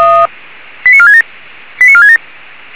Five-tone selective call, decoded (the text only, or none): 89467 89467